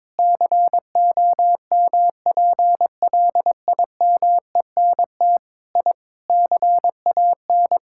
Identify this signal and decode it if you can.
Morse code, heard as a keyed tone, about 22 wpm, 700 Hz. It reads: COMPLIMENT I CAN
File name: practice / 22wpm / Sentences with 8-Word Phrases - 22wpm